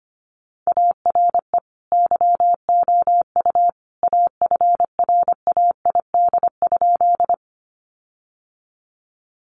Morse code, heard as a keyed tone, about 25 wpm, 700 Hz. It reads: ARE YOU AFRAID?
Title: Sentences from Top 1500 Words 25wpm